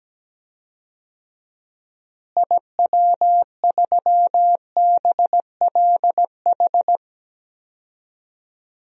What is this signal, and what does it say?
Morse code, heard as a keyed tone, about 17 wpm, 700 Hz. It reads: IW3BLH
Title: DXCC-100 Call Signs - 6 Characters - Repeated 3x - 17wpm